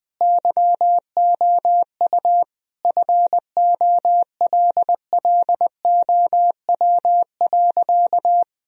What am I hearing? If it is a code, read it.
Morse code, heard as a keyed tone, about 20 wpm, 700 Hz. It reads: YOU FOLLOW.
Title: Sentences from Top 1000 Words - Review Part 3 - 20wpm